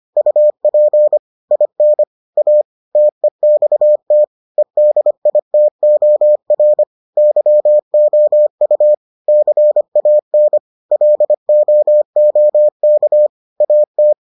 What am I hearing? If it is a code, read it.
Morse code, heard as a keyed tone, about 25 wpm, 600 Hz. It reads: UP IN A TEXT EDITOR YOU CAN LOOK AT